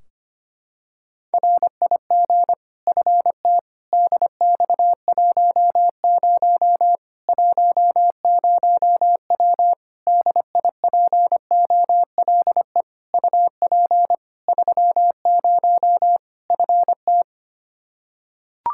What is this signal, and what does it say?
Morse code, heard as a keyed tone, about 25 wpm, 700 Hz. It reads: RIG FT DX10 10W DIPOLE UP 30 FT
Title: Practice QSO - Exchange 2 - Info - 25wpm